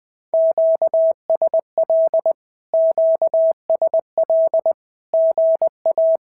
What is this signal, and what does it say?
Morse code, heard as a keyed tone, about 20 wpm, 690 Hz. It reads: QSL QSL GA
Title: SOTA QSO Practice 20wpm